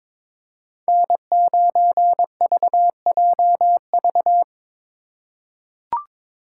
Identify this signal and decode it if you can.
Morse code, heard as a keyed tone, about 22 wpm, 700 Hz. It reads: N9VJV
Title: USA Call Signs - 5 Characters - 3x Character Spacing - 22wpm